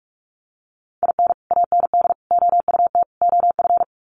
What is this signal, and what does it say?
Morse code, heard as a keyed tone, about 44 wpm, 700 Hz. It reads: IN AND OUT OF
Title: Sets of 4 Words 45wpm